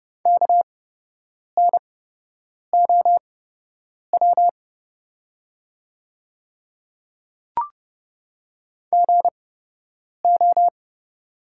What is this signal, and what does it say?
Morse code, heard as a keyed tone, about 31 wpm, spaced as for 8 wpm, 700 Hz. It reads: KNOW GO
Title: 8x Character Spacing - Top 300 Words - 30wpm